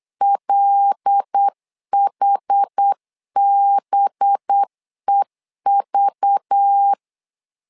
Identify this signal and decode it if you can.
Morse code, heard as a keyed tone, about 8 wpm, 800 Hz. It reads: LHBEV
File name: LEZIONI / 5 LBVC / 5 LBV 8wpm 2837